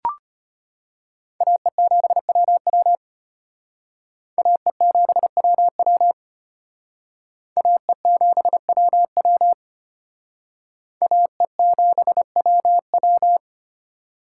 Morse code, heard as a keyed tone, about 31 wpm, 700 Hz. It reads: AE7WW AE7WW AE7WW AE7WW
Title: Speed Racing - USA 2x2 Call Signs 25wpm